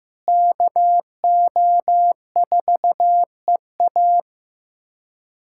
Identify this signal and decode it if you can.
Morse code, heard as a keyed tone, about 15 wpm, 700 Hz. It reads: KO4EA
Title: USA Callsign DE Callsign - 5 Characters - 15wpm